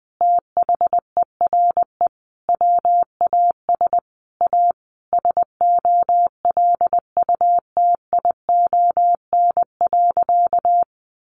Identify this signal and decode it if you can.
Morse code, heard as a keyed tone, about 20 wpm, 700 Hz. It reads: THERE WAS A SOLUTION.
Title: Sentences from Top 1300 Words 20wpm